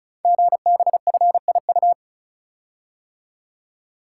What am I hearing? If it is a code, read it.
Morse code, heard as a keyed tone, about 36 wpm, 690 Hz. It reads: GBFIU
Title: Code Groups - 5 Letters - 35wpm